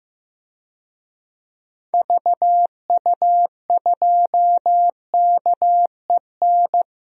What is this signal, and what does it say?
Morse code, heard as a keyed tone, about 16 wpm, 700 Hz. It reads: VU2KEN
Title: DXCC-100 Call Signs - 6 Characters - Repeated 3x - 15wpm